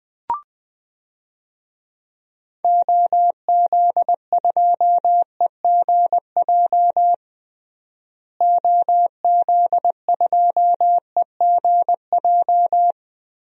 Morse code, heard as a keyed tone, about 20 wpm, 700 Hz. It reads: OZ2EGJ OZ2EGJ
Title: DXCC-100 Call Signs - 6 Characters - Repeated 2x - 20wpm